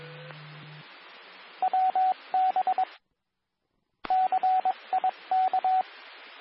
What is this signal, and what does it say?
Morse code, heard as a keyed tone, about 22 wpm, 750 Hz, with radio noise. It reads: WB CIK
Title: Nassau and Suffolk County Amateur Repeaters